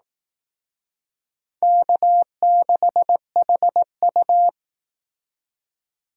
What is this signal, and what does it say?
Morse code, heard as a keyed tone, about 18 wpm, 700 Hz. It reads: K6HU
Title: Speed Racing - USA 1x2 Call Signs 15wpm